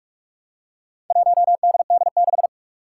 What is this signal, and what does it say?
Morse code, heard as a keyed tone, about 43 wpm, 700 Hz. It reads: 1DD6
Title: Four Challenging Letter-Number Combinations 45wpm